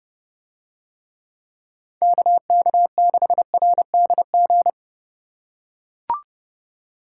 Morse code, heard as a keyed tone, about 30 wpm, 700 Hz. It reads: KK6RDG